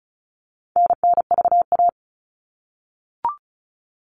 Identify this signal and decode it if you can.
Morse code, heard as a keyed tone, about 34 wpm, 700 Hz. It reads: NNVA
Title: All Cut Numbers - 4-Digit 35wpm